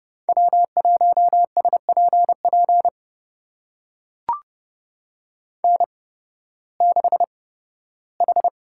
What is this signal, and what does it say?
Morse code, heard as a keyed tone, about 30 wpm, spaced as for 16 wpm, 700 Hz. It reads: W1SPP N6H
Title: USA Call Signs - 5 Characters - 8x Character Spacing - 30wpm